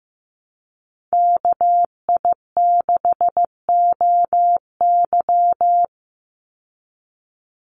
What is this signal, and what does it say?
Morse code, heard as a keyed tone, about 15 wpm, 700 Hz. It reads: KI6OY
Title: Call Signs Repeated 2x - 15wpm